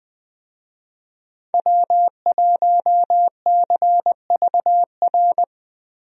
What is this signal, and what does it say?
Morse code, heard as a keyed tone, about 20 wpm, 700 Hz. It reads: W1CVR
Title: USA Call Signs - 5 Characters - 20wpm